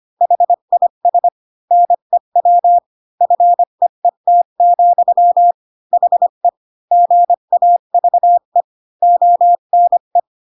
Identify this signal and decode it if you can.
Morse code, heard as a keyed tone, about 25 wpm, 700 Hz. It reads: HIS NEW FEET, HE GAVE ONE
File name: Pinocchio / Pinocchio ch8 20 25 ik1hge